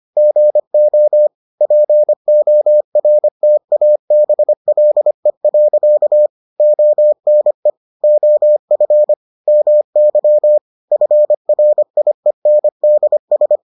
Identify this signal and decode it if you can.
Morse code, heard as a keyed tone, about 25 wpm, 600 Hz. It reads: GO PORTABLE. ONE OF MY FRIENDS